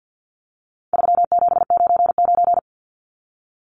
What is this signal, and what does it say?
Morse code, heard as a keyed tone, about 50 wpm, 700 Hz. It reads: 3799
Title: Numbers - 4-Digit 50wpm